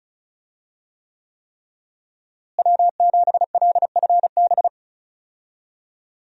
Morse code, heard as a keyed tone, about 36 wpm, 700 Hz. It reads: W7LFB
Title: USA Call Signs - 5 Characters - Repeated 3x - 35wpm